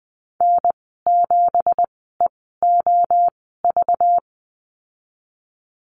Morse code, heard as a keyed tone, about 20 wpm, spaced as for 14 wpm, 700 Hz. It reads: N7EOV
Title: 213 Lesson 34g - Number 8 - Call Signs 20wpm